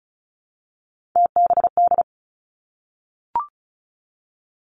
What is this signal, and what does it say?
Morse code, heard as a keyed tone, about 36 wpm, 700 Hz. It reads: TBD
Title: Three Challenging Letter-Number Combinations 35wpm